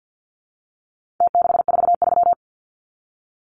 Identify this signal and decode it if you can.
Morse code, heard as a keyed tone, about 49 wpm, 700 Hz. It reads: T643
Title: Numbers - Contest Style - 4-Digit 50wpm